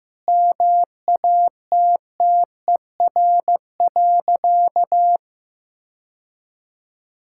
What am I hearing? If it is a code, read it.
Morse code, heard as a keyed tone, about 15 wpm, 700 Hz. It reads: MATTER.
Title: Sentences from Top 1000 Words - Review Part 5 - 15wpm